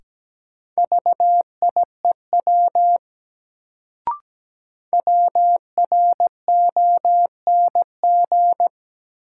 Morse code, heard as a keyed tone, about 17 wpm, 700 Hz. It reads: VIEW WRONG
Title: Sentences from Top 900 Words - New Word Drill - 17wpm